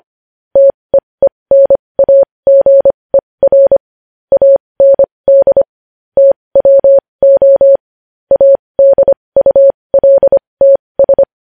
Morse code, heard as a keyed tone, about 25 wpm, spaced as for 20 wpm, 550 Hz. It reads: TEENAGER AND TWO ADULTS
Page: news-headlines-in-morse-code-at-20-wpm